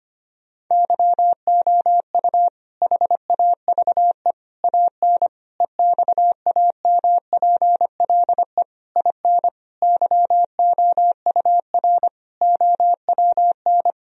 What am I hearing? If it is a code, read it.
Morse code, heard as a keyed tone, about 25 wpm, 700 Hz. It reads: YOU HAVE AN EXAMPLE IN YOUR OWN
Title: Sentences from Top 1000 Words - Review Part 1 - 25wpm